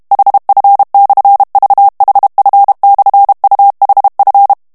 Morse code, heard as a keyed tone, about 32 wpm, 800 Hz. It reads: HF/VHF/UHF
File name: morse mp3 / HOLLANDAIS texte clair 1 vitesse 30 mots minute